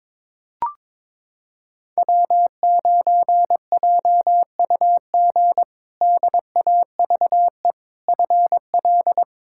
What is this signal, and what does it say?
Morse code, heard as a keyed tone, about 21 wpm, 700 Hz. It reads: W9JUG DAVE FL
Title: K1USN SST QSO Practice - 22wpm